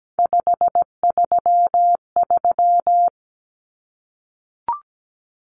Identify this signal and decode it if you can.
Morse code, heard as a keyed tone, about 17 wpm, 700 Hz. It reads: 533
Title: Numbers - 3-Digit 17wpm